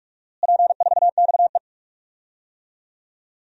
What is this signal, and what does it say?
Morse code, heard as a keyed tone, about 46 wpm, 700 Hz. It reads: P4XE
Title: DXCC-100 Call Signs - 4 Characters - Repeated 3x - 45wpm